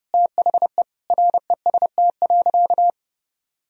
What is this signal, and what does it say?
Morse code, heard as a keyed tone, about 32 wpm, 700 Hz. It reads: THE REST.
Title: Sentences from Top 1000 Words - Review Part 1 - 30wpm